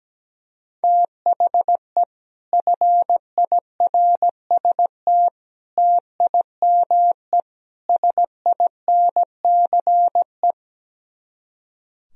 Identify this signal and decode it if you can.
Morse code, heard as a keyed tone, about 17 wpm, 700 Hz. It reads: THE FIRST TIME SINCE
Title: Speed-Racing - Sets of 4 Words 17wpm